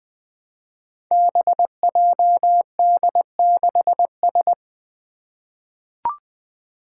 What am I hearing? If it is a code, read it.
Morse code, heard as a keyed tone, about 20 wpm, 700 Hz. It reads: BJD6S
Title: Five Challenging Letter-Number Combinations 20wpm